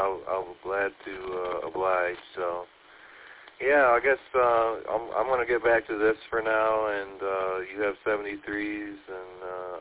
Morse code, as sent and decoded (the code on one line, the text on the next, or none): none